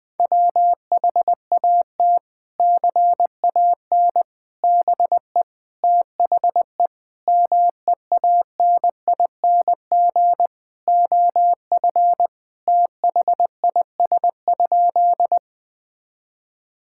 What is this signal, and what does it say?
Morse code, heard as a keyed tone, about 20 wpm, 700 Hz. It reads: WHAT CAN BE THE MEANING OF THIS?
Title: Sentences from Top 1000 Words - Review Part 6 - 20wpm